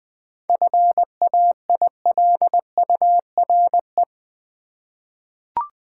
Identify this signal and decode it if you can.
Morse code, heard as a keyed tone, about 20 wpm, 700 Hz. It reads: FAILURE
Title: 7 Letter Words - First 3 Letters Given - 20wpm